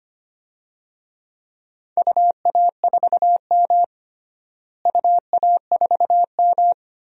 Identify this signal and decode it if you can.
Morse code, heard as a keyed tone, about 25 wpm, 700 Hz. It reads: UA4M UA4M
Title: DXCC-100 Call Signs - 4 Characters - Repeated 2x - 25wpm